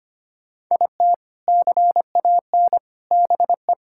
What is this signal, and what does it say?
Morse code, heard as a keyed tone, about 25 wpm, 700 Hz. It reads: IT CAN BE
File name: practice / 25wpm / Sentences with 2-Word Phrases - 25wpm